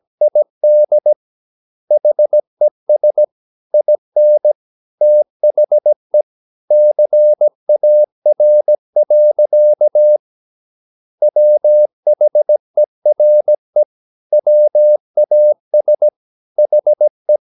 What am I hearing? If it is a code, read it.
Morse code, heard as a keyed tone, about 17 wpm, 600 Hz. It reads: ID HES IN THE CAR. WHERE WAS HE
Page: Romanen Never af Ken Follet på engelsk.